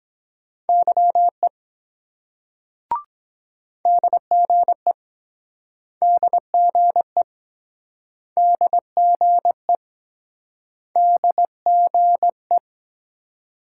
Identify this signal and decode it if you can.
Morse code, heard as a keyed tone, about 23 wpm, 700 Hz. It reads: YE DGE DGE DGE DGE